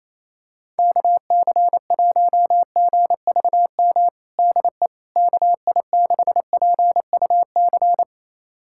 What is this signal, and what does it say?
Morse code, heard as a keyed tone, about 28 wpm, 700 Hz. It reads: KC1GVM DE KI6PUC
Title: USA Callsign DE Callsign 28wpm